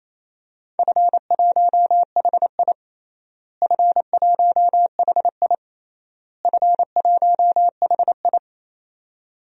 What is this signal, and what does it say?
Morse code, heard as a keyed tone, about 28 wpm, 700 Hz. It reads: F1HI F1HI F1HI
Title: DXCC-100 Call Signs - 4 Characters - Repeated 3x - 28wpm